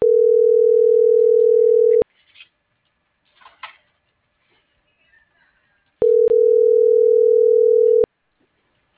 Ringback tone (North America), 440 + 480 Hz.